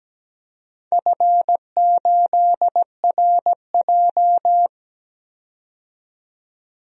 Morse code, heard as a keyed tone, about 17 wpm, 700 Hz. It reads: F8RJ
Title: DXCC-100 Call Signs - 4 Characters - 17wpm